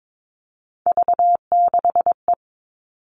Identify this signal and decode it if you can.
Morse code, heard as a keyed tone, about 22 wpm, 700 Hz. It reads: V6E
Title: All Cut Numbers - 3-Digit 22wpm